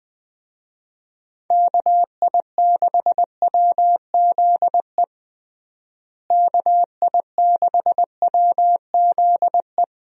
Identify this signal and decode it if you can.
Morse code, heard as a keyed tone, about 20 wpm, 700 Hz. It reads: KI6WZE KI6WZE
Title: USA Call Signs - 6 Characters - Repeated 2x - 20wpm